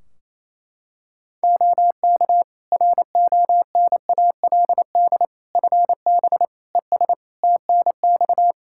Morse code, heard as a keyed tone, about 28 wpm, 700 Hz. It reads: OK RONALD FB ES TNX
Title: Practice QSO - Exchange 2 - Info - 28wpm